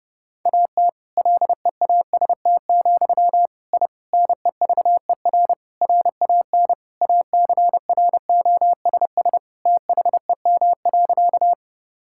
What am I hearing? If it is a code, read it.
Morse code, heard as a keyed tone, about 30 wpm, 700 Hz. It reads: AT LEAST, I NEVER RAN ACROSS THEM.
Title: Sentences from Top 1000 Words - Review Part 5 - 30wpm